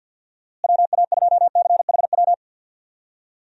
Morse code, heard as a keyed tone, about 46 wpm, 700 Hz. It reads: WA2CHW
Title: USA Call Signs - 6 Characters - Repeated 3x - 50wpm